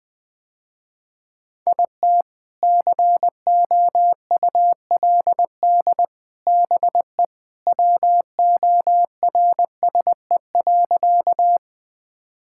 Morse code, heard as a keyed tone, about 20 wpm, 700 Hz. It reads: IT COULD BE WORSE.
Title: Sentences with 3-Word Phrases - Spoken Beginning - 20wpm